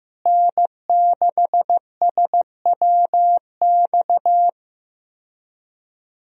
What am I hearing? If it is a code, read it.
Morse code, heard as a keyed tone, about 15 wpm, 700 Hz. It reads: N6SWX
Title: USA Call Signs - 5 Characters - Repeated 2x - 15wpm